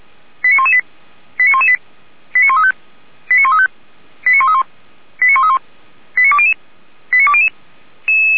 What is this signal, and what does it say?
Five-tone selective call (ZVEI), heard as quick runs of five tones, tones of about 70 ms. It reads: 89118 89118 89135 89135 89131 89131 89200 89200